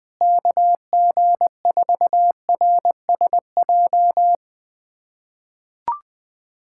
Morse code, heard as a keyed tone, about 20 wpm, 700 Hz. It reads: KG4RSJ
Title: USA Callsign DE Callsign - 6 Characters - 20wpm